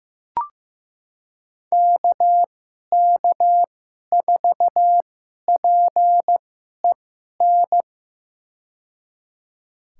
Morse code, heard as a keyed tone, about 15 wpm, spaced as for 11 wpm, 700 Hz. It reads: KK4PEN
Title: USA Call Signs - 2x Character Spacing - 15wpm